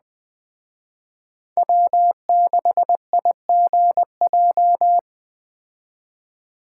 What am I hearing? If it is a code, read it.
Morse code, heard as a keyed tone, about 20 wpm, 700 Hz. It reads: W6IGJ